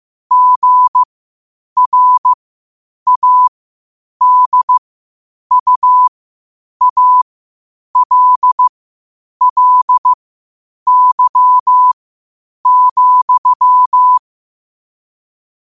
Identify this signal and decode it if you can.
Morse code, heard as a keyed tone, about 15 wpm, spaced as for 8 wpm, 1000 Hz. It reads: GRADUALLY,
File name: qotdfiles / 8wpm